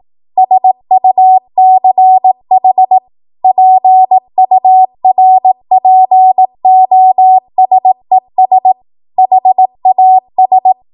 Morse code, heard as a keyed tone, about 18 wpm, 750 Hz. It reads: SUCH PURPOSES HAS
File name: Morse / 18WPM